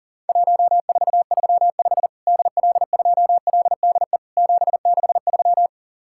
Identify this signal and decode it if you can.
Morse code, heard as a keyed tone, about 44 wpm, 700 Hz. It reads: 1435 DL2LDE 763
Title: Worked All Europe DX Contest - QTC Exchange - 40wpm